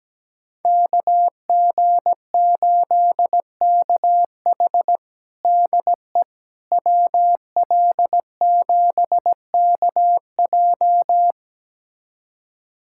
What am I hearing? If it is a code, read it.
Morse code, heard as a keyed tone, about 17 wpm, 700 Hz. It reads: KG8KH DE WL7KJ
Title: USA Callsign DE Callsign - 5 Characters - 17wpm